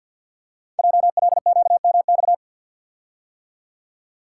Morse code, heard as a keyed tone, about 50 wpm, 700 Hz. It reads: JLQMX
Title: Code Groups - 5 Letters - 50wpm